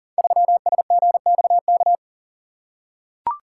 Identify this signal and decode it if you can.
Morse code, heard as a keyed tone, about 41 wpm, 700 Hz. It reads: 3SGXK
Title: Code Groups - 5 Letter:Numbers - 40wpm